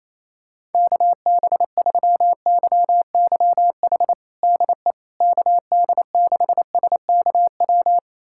Morse code, heard as a keyed tone, about 28 wpm, 700 Hz. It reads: KB3YYH DE KD6SKW
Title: USA Callsign DE Callsign - 6 Characters - 28wpm